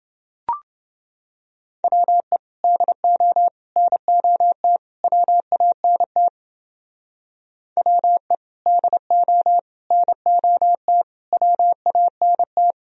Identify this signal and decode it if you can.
Morse code, heard as a keyed tone, about 29 wpm, 700 Hz. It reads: WE DO NOT WANT WE DO NOT WANT